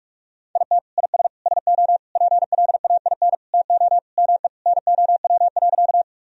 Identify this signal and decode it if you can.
Morse code, heard as a keyed tone, about 42 wpm, 700 Hz. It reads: IT IS SO PLAIN TO ME NOW.
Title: Sentences from Top 1000 Words - Review Part 2 - 45wpm